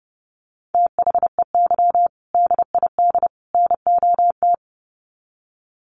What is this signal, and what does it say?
Morse code, heard as a keyed tone, about 30 wpm, 700 Hz. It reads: THEY DID NOT
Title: Sets of 3 Words 30wpm